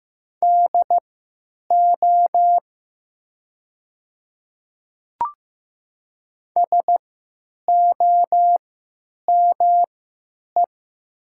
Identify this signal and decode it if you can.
Morse code, heard as a keyed tone, about 15 wpm, spaced as for 9 wpm, 700 Hz. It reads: DO SOME
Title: 3x Character Spacing - Top 200 Words - 15wpm